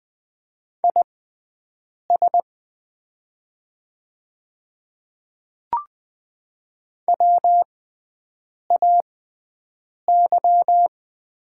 Morse code, heard as a keyed tone, about 20 wpm, spaced as for 7 wpm, 700 Hz. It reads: IS WAY
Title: 6x Character Spacing - Top 100 Words - 20wpm